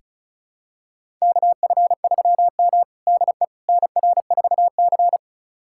Morse code, heard as a keyed tone, about 34 wpm, 700 Hz. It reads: KF3M DE NR4C
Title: USA Callsign DE Callsign - 4 Characters - 35wpm